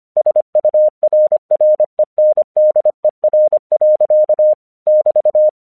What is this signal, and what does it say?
Morse code, heard as a keyed tone, about 25 wpm, 610 Hz. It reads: SURRENDER. <BT>
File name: 1 English 25wpm / AQC09